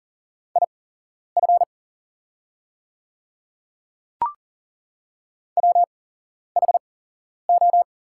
Morse code, heard as a keyed tone, about 40 wpm, spaced as for 11 wpm, 690 Hz. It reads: IF WHO
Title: 8x Character Spacing - Top 300 Words - 40wpm